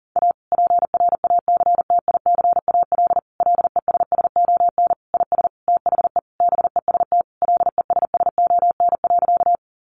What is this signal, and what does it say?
Morse code, heard as a keyed tone, about 40 wpm, 700 Hz. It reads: A PRACTICAL LESSON IS THE BEST LESSON.
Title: Sentences from Top 1400 Words 40wpm